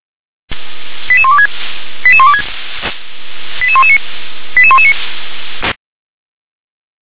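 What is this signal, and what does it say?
Five-tone selective call (ZVEI), heard as quick runs of five tones, tones of about 70 ms. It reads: 88126 88126 88119 88119
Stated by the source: Received by SDR as ZVEI five-tone selective call 88786.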